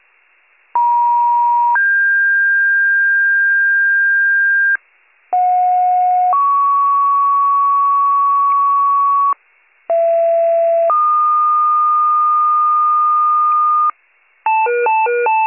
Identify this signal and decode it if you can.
Two-tone page, heard as a one-second tone, then a three-second tone, about 950 then 1600 Hz, about 730 then 1120 Hz, about 670 then 1220 Hz. It is heard from 0.7 to 13.9 s.